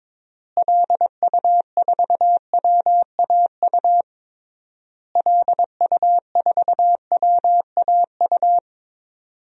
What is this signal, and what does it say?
Morse code, heard as a keyed tone, about 21 wpm, 700 Hz. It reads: LU4WAU LU4WAU
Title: DXCC-100 Call Signs - 6 Characters - Repeated 3x - 22wpm